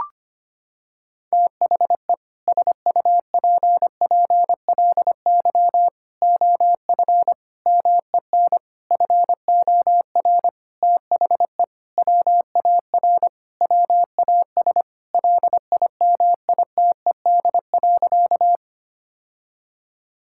Morse code, heard as a keyed tone, about 25 wpm, 700 Hz. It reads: THE SUPPLY OF MEN FOR THE WAR WAS LIMITED.
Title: Sentences from Top 1000 Words - Review Part 6 - 25wpm